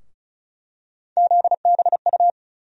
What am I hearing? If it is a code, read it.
Morse code, heard as a keyed tone, about 36 wpm, 700 Hz. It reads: ZBU
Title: USA Call Sign Postfixes - 3 Letters 35wpm